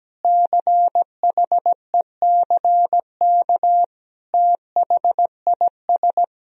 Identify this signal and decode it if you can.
Morse code, heard as a keyed tone, about 17 wpm, 700 Hz. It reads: CHECK THIS
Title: Sentences from Top 1000 Words - Review Part 3 - 17wpm